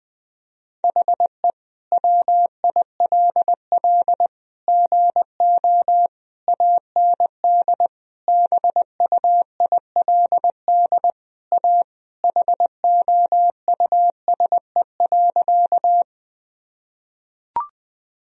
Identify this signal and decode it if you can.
Morse code, heard as a keyed tone, about 20 wpm, 700 Hz. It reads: HE WILL GO AND BUILD A HOUSE.
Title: Sentences from Top 1000 Words - Review Part 1 - 20wpm